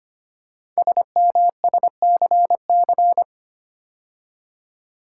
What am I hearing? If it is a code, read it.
Morse code, heard as a keyed tone, about 25 wpm, 700 Hz. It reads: SMSCC
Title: Code Groups - 5 Letters - 25wpm